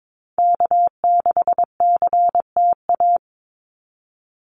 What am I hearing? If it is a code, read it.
Morse code, heard as a keyed tone, about 22 wpm, 700 Hz. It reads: K6CTA
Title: Call Signs Repeated 2x - 22wpm